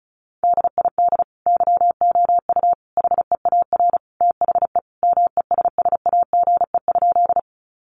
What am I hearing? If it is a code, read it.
Morse code, heard as a keyed tone, about 34 wpm, 700 Hz. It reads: DID YOU HEAR THE MESSAGE?
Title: Sentences from Top 1100 Words 35wpm